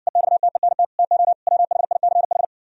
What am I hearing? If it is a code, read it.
Morse code, heard as a keyed tone, about 69 wpm, 700 Hz. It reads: EXTENT TO WHICH